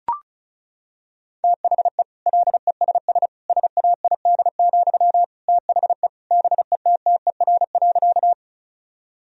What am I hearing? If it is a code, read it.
Morse code, heard as a keyed tone, about 36 wpm, 700 Hz. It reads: THE LESS SAID, THE BETTER.
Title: Sentences from Top 1000 Words - Review Part 2 - 35wpm